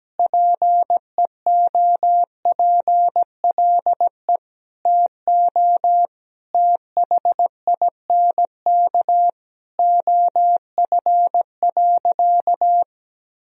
Morse code, heard as a keyed tone, about 17 wpm, 700 Hz. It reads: PEOPLE TO THINK OF.
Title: Sentences from Top 1000 Words - Review Part 1 - 17wpm